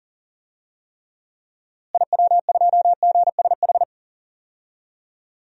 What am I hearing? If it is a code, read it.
Morse code, heard as a keyed tone, about 40 wpm, 700 Hz. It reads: IW2GSH
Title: DXCC-100 Call Signs - 6 Characters - Repeated 3x - 40wpm